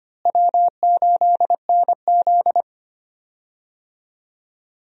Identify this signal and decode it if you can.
Morse code, heard as a keyed tone, about 25 wpm, 700 Hz. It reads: W8NZ
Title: USA Call Signs - 4 Characters - Repeated 3x - 25wpm